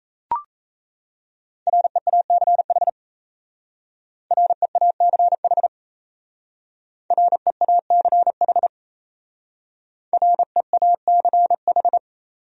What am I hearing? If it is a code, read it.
Morse code, heard as a keyed tone, about 35 wpm, 700 Hz. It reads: REACH REACH REACH REACH
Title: Speed-Racing Top 500 Words 28wpm